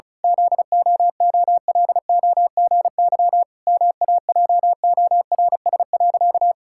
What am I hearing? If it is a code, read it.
Morse code, heard as a keyed tone, about 36 wpm, 700 Hz. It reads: ZOOLOGY MAJORS.
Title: Pangram Challenge v2 - 35wpm